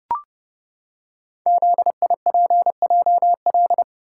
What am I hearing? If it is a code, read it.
Morse code, heard as a keyed tone, about 30 wpm, 700 Hz. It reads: ZIPJL
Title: Code Groups - 5 Letter:Numbers - 30wpm